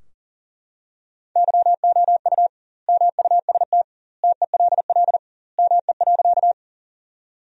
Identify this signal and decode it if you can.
Morse code, heard as a keyed tone, about 40 wpm, 700 Hz. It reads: YOU MUST TELL ME.
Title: 2x Word-Spacing - Sentences from Top 200 Words - 40wpm